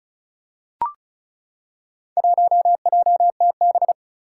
Morse code, heard as a keyed tone, about 34 wpm, 700 Hz. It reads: 1JTB
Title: Four Challenging Letter-Number Combinations 35wpm